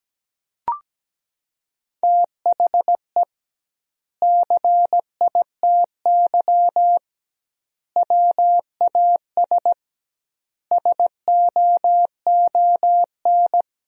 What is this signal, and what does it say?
Morse code, heard as a keyed tone, about 17 wpm, 700 Hz. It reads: THE CITY WAS SOON
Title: 2x Word-Spacing - Sentences from Top 500 Words - 17wpm